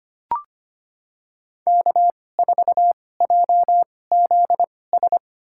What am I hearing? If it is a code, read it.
Morse code, heard as a keyed tone, about 25 wpm, spaced as for 18 wpm, 700 Hz. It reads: K4JZS
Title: USA Call Signs - 5 Characters - 2x Character Spacing - 25wpm